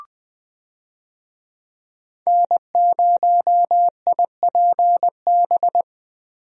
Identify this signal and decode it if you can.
Morse code, heard as a keyed tone, about 20 wpm, 700 Hz. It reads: N0IPB
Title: USA Call Signs - 5 Characters - Repeated 2x - 20wpm